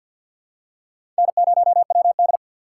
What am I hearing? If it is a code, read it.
Morse code, heard as a keyed tone, about 52 wpm, 700 Hz. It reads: N0WD
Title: USA Call Signs - 4 Characters - 2x Character Spacing - 50wpm